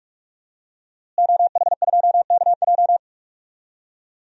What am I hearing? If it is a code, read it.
Morse code, heard as a keyed tone, about 46 wpm, 700 Hz. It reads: OH2KJ
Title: DXCC-100 Call Signs - 5 Characters - Repeated 2x - 45wpm